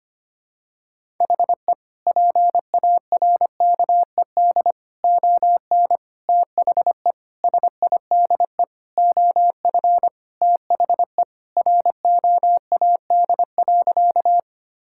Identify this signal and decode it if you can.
Morse code, heard as a keyed tone, about 25 wpm, 700 Hz. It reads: HE PARKED ON THE SIDE OF THE ROAD.
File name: practice / 25wpm / Sentences with 4-Word Phrases - 25wpm